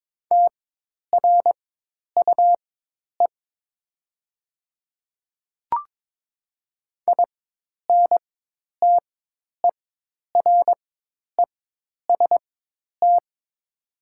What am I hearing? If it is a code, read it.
Morse code, heard as a keyed tone, about 22 wpm, spaced as for 10 wpm, 700 Hz. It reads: TRUE INTEREST
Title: 4x Character Spacing - Top 500 Words - 22wpm